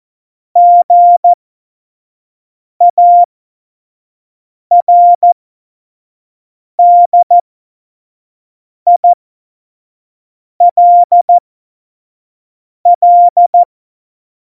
Morse code, heard as a keyed tone, about 14 wpm, spaced as for 5 wpm, 700 Hz. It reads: GARDILL
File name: Morsekurs / Witze / Witz 03. November 2020, 25 F70